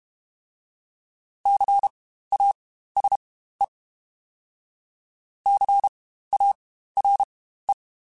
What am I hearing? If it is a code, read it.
Morse code, heard as a keyed tone, about 32 wpm, spaced as for 15 wpm, 800 Hz. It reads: CASE CARE
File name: Spaced Alphas / Spaced Alphas mp3s / Cs sp @30